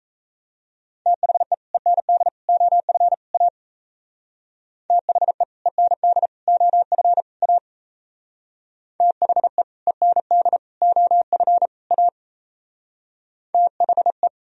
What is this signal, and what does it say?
Morse code, heard as a keyed tone, about 34 wpm, 700 Hz. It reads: THE END OF A THE END OF A THE END OF A THE
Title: Speed-Racing - Sets of 4 Words 28wpm